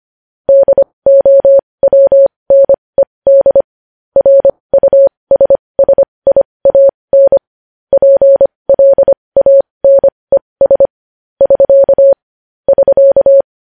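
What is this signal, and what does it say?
Morse code, heard as a keyed tone, about 25 wpm, spaced as for 20 wpm, 550 Hz. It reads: DOWNED RUSSIAN PLANES <SK> <SK>
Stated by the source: Morse code sending news headlines at 20 wpm.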